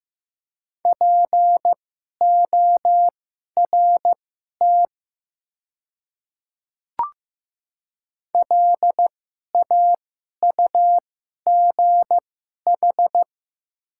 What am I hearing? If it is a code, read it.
Morse code, heard as a keyed tone, about 15 wpm, spaced as for 11 wpm, 700 Hz. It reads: PORT LAUGH
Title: 2x Character Spacing - Top 500 Words - 15wpm